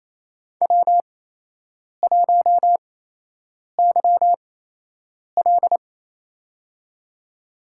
Morse code, heard as a keyed tone, about 28 wpm, spaced as for 8 wpm, 700 Hz. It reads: W1YL
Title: USA Call Signs - 4 Characters - 8x Character Spacing - 28wpm